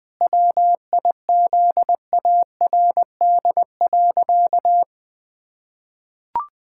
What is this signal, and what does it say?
Morse code, heard as a keyed tone, about 20 wpm, 700 Hz. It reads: WIZARD.
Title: Pangram Challenge v2 - 20wpm